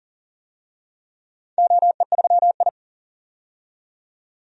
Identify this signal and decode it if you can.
Morse code, heard as a keyed tone, about 38 wpm, 700 Hz. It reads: OE3I
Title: DXCC-100 Call Signs - 4 Characters - 40wpm